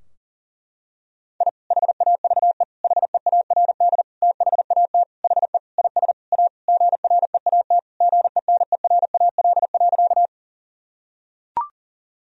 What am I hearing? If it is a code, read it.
Morse code, heard as a keyed tone, about 40 wpm, 700 Hz. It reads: I HAVE HEARD THAT HE IS A GREAT GENERAL.
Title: Sentences from Top 1000 Words - Review Part 3 - 40wpm